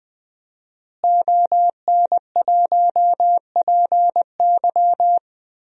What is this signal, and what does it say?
Morse code, heard as a keyed tone, about 20 wpm, 700 Hz. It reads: ON1PY
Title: DXCC-100 Call Signs - 5 Characters - Repeated 2x - 20wpm